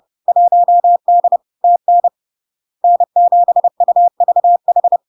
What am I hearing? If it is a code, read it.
Morse code, heard as a keyed tone, about 30 wpm, 700 Hz. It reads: 1D TN N7UVH
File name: LARK-FD-2022-Log-in-CW-30wpm-700-Hz